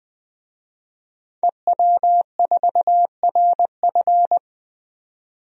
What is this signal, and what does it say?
Morse code, heard as a keyed tone, about 21 wpm, 700 Hz. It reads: EW4RF
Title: DXCC-100 Call Signs - 5 Characters - 20wpm